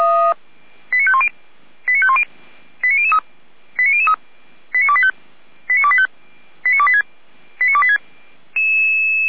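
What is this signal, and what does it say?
Five-tone selective call (ZVEI), heard as quick runs of five tones, tones of about 80 ms. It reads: 89510 89510 89003 89003 89285 89285 89286 89286
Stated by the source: Received on an SDR.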